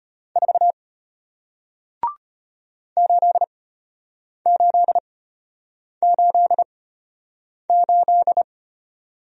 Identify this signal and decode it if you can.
Morse code, heard as a keyed tone, about 33 wpm, spaced as for 8 wpm, 700 Hz. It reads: 4 8888